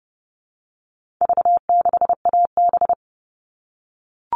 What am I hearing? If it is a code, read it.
Morse code, heard as a keyed tone, about 30 wpm, 700 Hz. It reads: V6AB
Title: All Cut Numbers - 4-Digit 30wpm